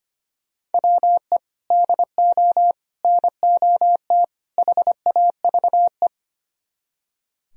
Morse code, heard as a keyed tone, about 30 wpm, 700 Hz. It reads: WE DO NOT HAVE
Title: Speed-Racing - Sets of 4 Words 25wpm